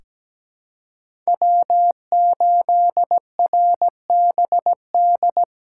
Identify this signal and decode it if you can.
Morse code, heard as a keyed tone, about 17 wpm, 700 Hz. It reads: W8RBD
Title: USA Call Signs - 5 Characters - 2x Character Spacing - 17wpm